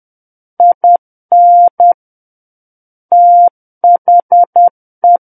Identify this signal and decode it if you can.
Morse code, heard as a keyed tone, about 10 wpm, 700 Hz. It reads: IN THE
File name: Book One / 01-Chapter 1 - 10 WPM